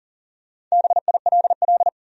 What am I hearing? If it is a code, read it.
Morse code, heard as a keyed tone, about 40 wpm, 700 Hz. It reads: BILL
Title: USA Common Names - Max 4 Letters - 40wpm